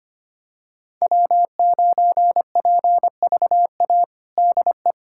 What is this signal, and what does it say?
Morse code, heard as a keyed tone, about 25 wpm, 700 Hz. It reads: W9PVA DE
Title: USA Callsign DE Callsign - 5 Characters - 25wpm